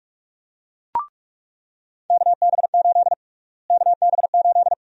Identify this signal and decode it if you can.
Morse code, heard as a keyed tone, about 46 wpm, 700 Hz. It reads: KB8 KB8
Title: USA Call Sign Prefixes - Repeated 2x - 45wpm